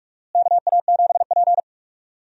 Morse code, heard as a keyed tone, about 47 wpm, 700 Hz. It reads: KA7P
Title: USA Call Signs - 4 Characters - Repeated 3x - 45wpm